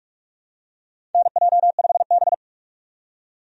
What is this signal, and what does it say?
Morse code, heard as a keyed tone, about 46 wpm, 700 Hz. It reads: NJ5B
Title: USA Call Signs - 4 Characters - 45wpm